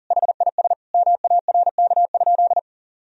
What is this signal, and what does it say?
Morse code, heard as a keyed tone, about 40 wpm, 700 Hz. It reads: HIS MARK?
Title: Sentences from Top 300 Words and Abbreviations 40wpm